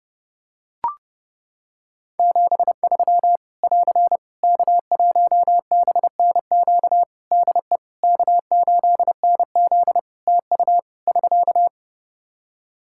Morse code, heard as a keyed tone, about 28 wpm, 700 Hz. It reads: 73 <AR> K1BNQ DE K8NZ TU <SK>
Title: Practice QSO - Exchange 4 - Ending - 30wpm